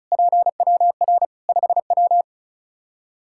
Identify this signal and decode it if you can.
Morse code, heard as a keyed tone, about 36 wpm, 700 Hz. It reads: PWR 5W
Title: Practice QSO - Exchange 2 - Info - 35wpm